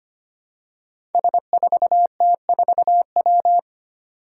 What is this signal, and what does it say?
Morse code, heard as a keyed tone, about 25 wpm, 700 Hz. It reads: S4T4W
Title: Five Challenging Letter-Number Combinations 25wpm